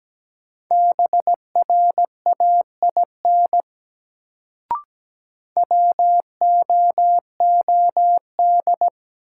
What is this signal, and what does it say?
Morse code, heard as a keyed tone, about 17 wpm, 700 Hz. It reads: BRAIN WOOD
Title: Sentences from Top 1100 Words - New Word Drill Alt - 17wpm